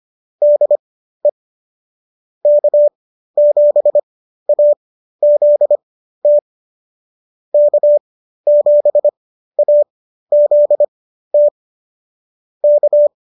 Morse code, heard as a keyed tone, about 25 wpm, spaced as for 13 wpm, 590 Hz. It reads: DE K7AZT K7AZT K